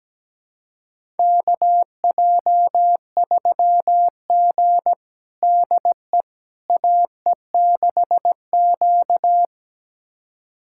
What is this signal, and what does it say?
Morse code, heard as a keyed tone, about 17 wpm, 700 Hz. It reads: KJ3G DE AE6Q
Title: USA Callsign DE Callsign - 4 Characters - 17wpm